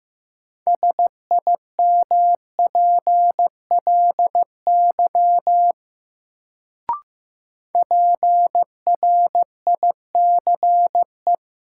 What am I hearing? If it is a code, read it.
Morse code, heard as a keyed tone, about 15 wpm, 700 Hz. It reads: SIMPLY PRICE
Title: Sentences from Top 600 Words - New Word Drill - 15wpm